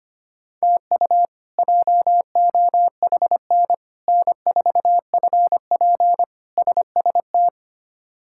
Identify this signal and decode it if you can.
Morse code, heard as a keyed tone, about 25 wpm, 700 Hz. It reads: TU JOHN N4FP SST
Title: K1USN SST QSO Practice - 25wpm